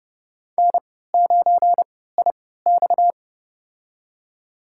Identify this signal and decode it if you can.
Morse code, heard as a keyed tone, about 30 wpm, spaced as for 17 wpm, 700 Hz. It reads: N9IX
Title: USA Call Signs - 4 Characters - 3x Character Spacing - 30wpm